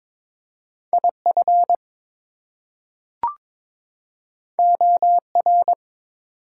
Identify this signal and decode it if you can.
Morse code, heard as a keyed tone, about 22 wpm, 700 Hz. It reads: IF OR
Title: Two Letter Words - 22wpm